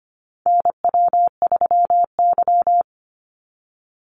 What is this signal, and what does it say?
Morse code, heard as a keyed tone, about 25 wpm, 700 Hz. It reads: NW3Y
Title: Call Signs - 25wpm